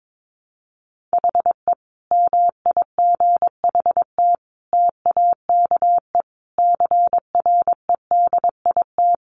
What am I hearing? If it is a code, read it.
Morse code, heard as a keyed tone, about 22 wpm, 700 Hz. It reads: HE MIGHT TAKE CREDIT
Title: Sentences from Top 1000 Words 22wpm